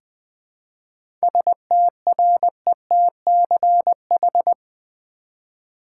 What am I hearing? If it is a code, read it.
Morse code, heard as a keyed tone, about 20 wpm, 700 Hz. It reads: STRETCH
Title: Seven Letter Words - 20wpm